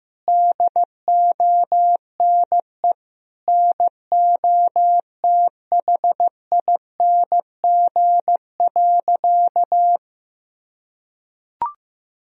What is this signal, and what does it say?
Morse code, heard as a keyed tone, about 15 wpm, 700 Hz. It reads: DONE NOTHING.
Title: Sentences from Top 1000 Words - Review Part 3 - 15wpm